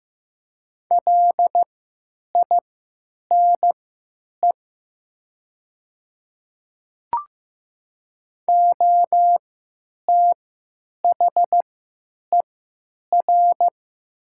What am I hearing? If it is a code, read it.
Morse code, heard as a keyed tone, about 15 wpm, spaced as for 9 wpm, 700 Hz. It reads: LINE OTHER
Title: 3x Character Spacing - Top 300 Words - 15wpm